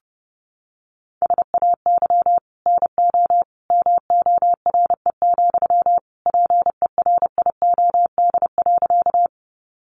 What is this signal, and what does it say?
Morse code, heard as a keyed tone, about 30 wpm, 700 Hz. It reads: SAY NO MORE, PERIOD.
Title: Sentences from Top 800 Words 30wpm